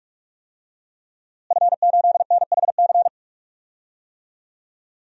Morse code, heard as a keyed tone, about 46 wpm, 700 Hz. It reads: F8NHC
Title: DXCC-100 Call Signs - 5 Characters - Repeated 3x - 45wpm